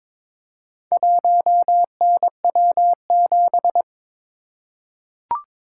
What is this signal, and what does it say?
Morse code, heard as a keyed tone, about 22 wpm, 700 Hz. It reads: 1NW7
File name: practice / 22wpm / Four Letter-Number Combinations - 22wpm